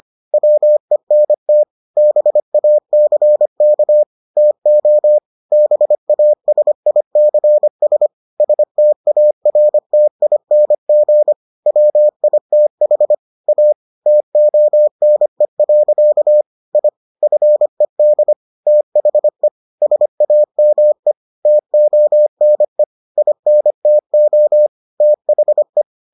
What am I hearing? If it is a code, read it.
Morse code, heard as a keyed tone, about 25 wpm, 600 Hz. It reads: WENT BACK TO BASICS STARTING WITH A TONE. I FED THE SAME TONE INTO THE